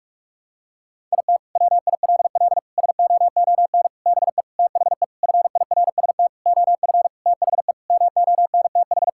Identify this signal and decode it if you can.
Morse code, heard as a keyed tone, about 46 wpm, 700 Hz. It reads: IT WILL SOON BE THE FIRST OF THE MONTH
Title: Sentences from Top 1000 Words - Review Part 5 - 45wpm